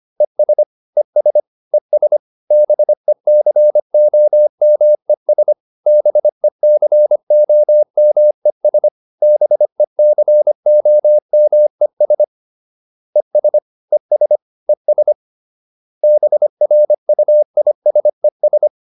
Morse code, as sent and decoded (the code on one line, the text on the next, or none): . ... / . ... / . ... / -... -.-. --- -- . ... / -... -.-. --- -- . ... / -... -.-. --- -- . ... / . ... / . ... / . ... / -... .-. ..- .. ... ...
ES ES ES BECOMES BECOMES BECOMES ES ES ES BRUISES